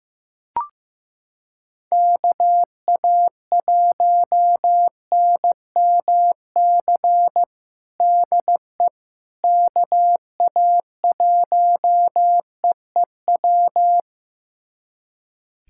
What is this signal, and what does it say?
Morse code, heard as a keyed tone, about 15 wpm, 700 Hz. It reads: KA1NMC DE KA1EEW
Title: USA Callsign DE Callsign 15wpm